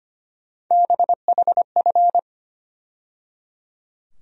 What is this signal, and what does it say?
Morse code, heard as a keyed tone, about 25 wpm, 700 Hz. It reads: BHF
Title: USA Call Sign Postfixes - 3 Letters 25wpm